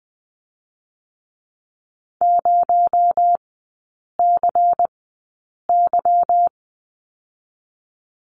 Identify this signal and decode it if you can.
Morse code, heard as a keyed tone, about 20 wpm, 700 Hz. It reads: 0 C Y
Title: 218 Lesson 35e - Number 0 - Three Characters 20wpm